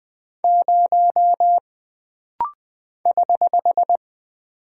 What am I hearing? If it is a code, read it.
Morse code, heard as a keyed tone, about 20 wpm, 700 Hz. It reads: 0 <HH>
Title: Single Character (Letter:Number:Punctuation:Prosign) - Rapid-Fire - 20wpm